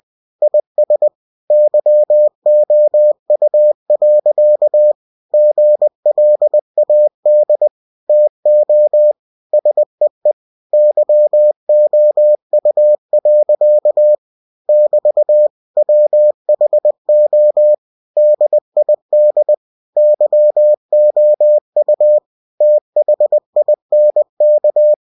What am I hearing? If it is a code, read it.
Morse code, heard as a keyed tone, about 20 wpm, 600 Hz. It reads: IS YOU. GLAD TO SEE YOU. <BT> WHO DID YOU THINK
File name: cw-books / winnie-the-pooh / winnie-20wpm-page-0015